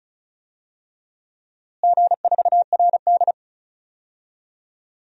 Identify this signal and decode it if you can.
Morse code, heard as a keyed tone, about 36 wpm, 700 Hz. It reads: G4RD
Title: DXCC-100 Call Signs - 4 Characters - Repeated 3x - 35wpm